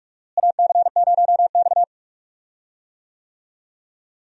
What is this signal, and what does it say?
Morse code, heard as a keyed tone, about 43 wpm, 700 Hz. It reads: AC0X
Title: USA Call Signs - 4 Characters - Repeated 3x - 45wpm